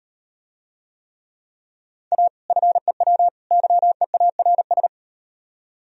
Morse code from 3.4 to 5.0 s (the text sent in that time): YEARS